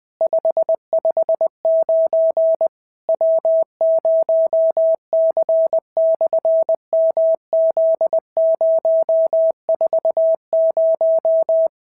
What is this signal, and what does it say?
Morse code, heard as a keyed tone, about 20 wpm, 650 Hz. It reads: 559 W0C/MZ040
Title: SOTA QSO Practice 20wpm